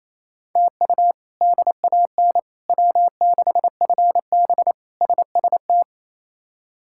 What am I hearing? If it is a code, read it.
Morse code, heard as a keyed tone, about 28 wpm, 700 Hz. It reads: TU DAN W6FB SST
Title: K1USN SST QSO Practice - 28wpm